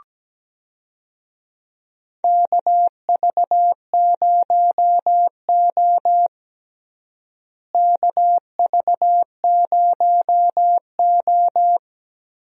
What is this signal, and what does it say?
Morse code, heard as a keyed tone, about 17 wpm, 700 Hz. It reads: KV0O KV0O
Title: USA Call Signs - 4 Characters - Repeated 2x - 17wpm